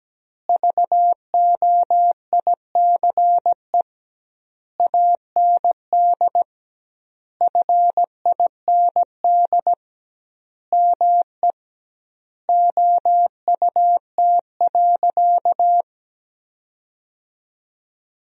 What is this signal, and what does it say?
Morse code, heard as a keyed tone, about 17 wpm, 700 Hz. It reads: VOICE AND FIND ME OUT.
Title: 2x Word-Spacing - Sentences from Top 500 Words - 17wpm